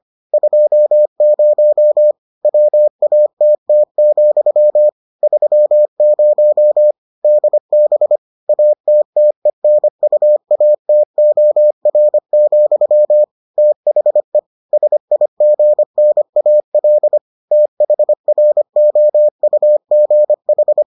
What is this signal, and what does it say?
Morse code, heard as a keyed tone, about 25 wpm, 600 Hz. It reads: 20 WATT, 30 DB ATTENUATOR, THE SIGNAL THROUGH